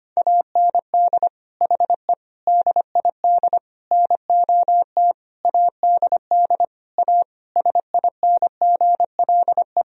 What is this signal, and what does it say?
Morse code, heard as a keyed tone, about 25 wpm, 700 Hz. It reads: AND HE DID NOT ADD A SINGLE
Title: Sentences from Top 1000 Words - Review Part 1 - 25wpm